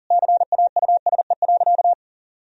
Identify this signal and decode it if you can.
Morse code, heard as a keyed tone, about 37 wpm, 700 Hz. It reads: CAUSE.
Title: Sentences from Top 1000 Words - Review Part 4 - 40wpm